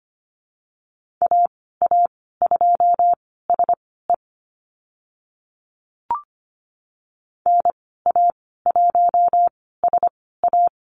Morse code, heard as a keyed tone, about 25 wpm, spaced as for 16 wpm, 700 Hz. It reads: AA2SE NA1SA
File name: Lesson 15g - Number 5 - Call Signs 25wpm